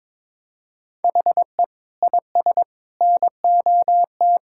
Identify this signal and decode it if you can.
Morse code, heard as a keyed tone, about 22 wpm, 700 Hz. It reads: HE IS NOT
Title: Sentences from Top 1000 Words - Review Part 5 - 22wpm